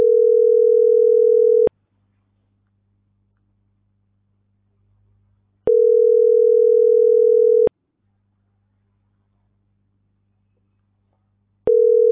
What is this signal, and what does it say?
Ringback tone (North America), 440 + 480 Hz.